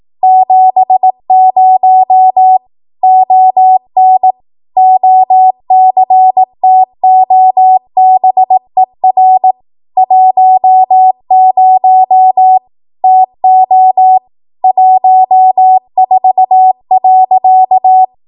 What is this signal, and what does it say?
Morse code, heard as a keyed tone, about 18 wpm, 750 Hz. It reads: 70 ON OCTOBER 10 TO 14.
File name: Archive / 18 WPM / 18WPM